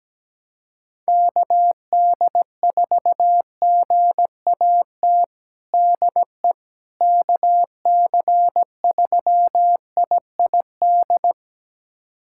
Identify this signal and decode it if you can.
Morse code, heard as a keyed tone, about 17 wpm, 700 Hz. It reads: KD4GAT DE KC3IID
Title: USA Callsign DE Callsign - 6 Characters - 17wpm